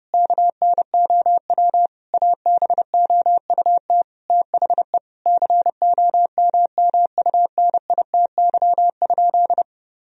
Morse code, heard as a keyed tone, about 30 wpm, 700 Hz. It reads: KNOW ABOUT THE COMMUNITY?
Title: Sentences from Top 1000 Words - Review Part 4 - 30wpm